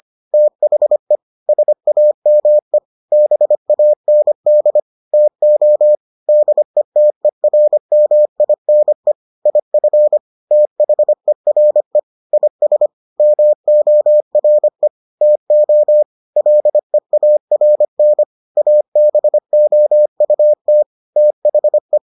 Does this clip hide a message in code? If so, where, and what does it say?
Yes: Morse code from 0.3 to 22.0 s, heard as a keyed tone, about 25 wpm, 600 Hz. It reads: THE SAME BAND TO DETERMINE IF THERE IS MORE TO LEARN ABOUT THE